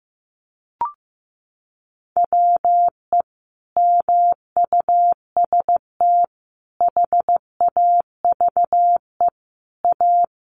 Morse code, heard as a keyed tone, about 15 wpm, 700 Hz. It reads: WE MUST HAVE A
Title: Sentences from Top 1500 Words 15wpm